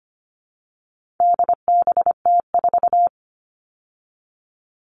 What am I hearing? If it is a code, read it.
Morse code, heard as a keyed tone, about 25 wpm, 700 Hz. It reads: DBT4
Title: All Cut Numbers - 4-Digit 25wpm